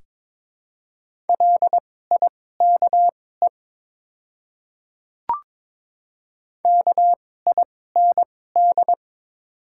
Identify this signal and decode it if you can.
Morse code, heard as a keyed tone, about 22 wpm, spaced as for 16 wpm, 700 Hz. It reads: LIKE KIND
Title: 2x Character Spacing - Top 200 Words - 22wpm